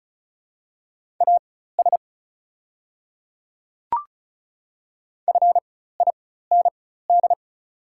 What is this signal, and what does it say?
Morse code, heard as a keyed tone, about 36 wpm, spaced as for 16 wpm, 700 Hz. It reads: AS FIND
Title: 4x Character Spacing - Top 200 Words - 35wpm